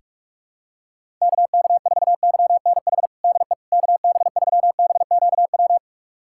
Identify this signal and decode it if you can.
Morse code, heard as a keyed tone, about 46 wpm, 700 Hz. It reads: KK4YNH DE KB3BQW
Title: USA Callsign DE Callsign - 6 Characters - 45wpm